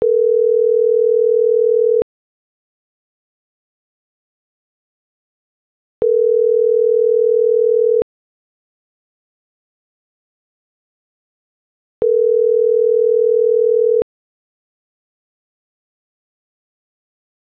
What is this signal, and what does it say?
Ringback tone (North America), 440 + 480 Hz.